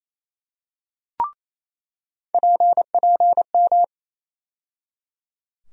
Morse code, heard as a keyed tone, about 28 wpm, 700 Hz. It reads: PPM
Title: USA Call Sign Postfixes - 3 Letters 28wpm